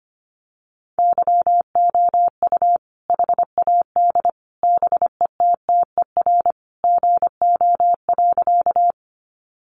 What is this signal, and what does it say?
Morse code, heard as a keyed tone, about 25 wpm, 700 Hz. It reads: YOU HAD BETTER GO.
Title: Sentences from Top 400 Words 25wpm